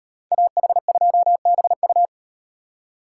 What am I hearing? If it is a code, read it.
Morse code, heard as a keyed tone, about 35 wpm, 700 Hz. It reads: AH2BU